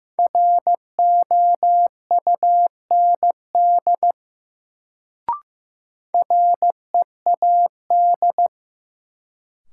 Morse code, heard as a keyed tone, about 16 wpm, 700 Hz. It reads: ROUND READ
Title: Sentences from Top 200 Words - New Word Drill - 15wpm